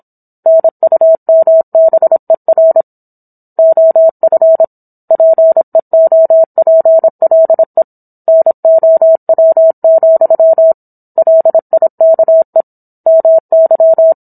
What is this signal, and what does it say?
Morse code, heard as a keyed tone, about 27 wpm, 650 Hz. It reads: NUMBER OF PEOPLE NOW, LIKE MY